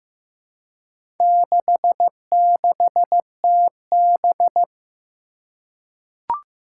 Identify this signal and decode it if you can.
Morse code, heard as a keyed tone, about 15 wpm, 700 Hz. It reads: 66TB
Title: Four Challenging Letter-Number Combinations 15wpm